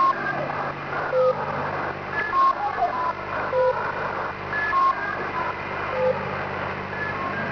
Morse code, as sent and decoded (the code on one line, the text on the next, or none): none